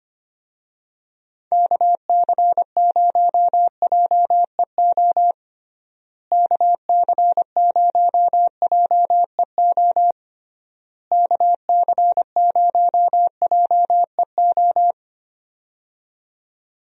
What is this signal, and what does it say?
Morse code, heard as a keyed tone, about 25 wpm, 700 Hz. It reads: KC0JEO KC0JEO KC0JEO
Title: USA Call Signs - 6 Characters - Repeated 3x - 25wpm